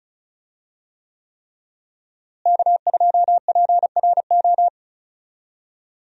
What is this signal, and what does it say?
Morse code, heard as a keyed tone, about 36 wpm, 700 Hz. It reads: K2PRO
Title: USA Call Signs - 5 Characters - Repeated 3x - 35wpm